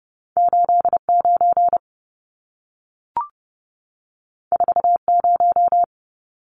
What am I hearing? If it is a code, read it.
Morse code, heard as a keyed tone, about 30 wpm, 700 Hz. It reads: 89 40
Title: Numbers - 2-Digit 30wpm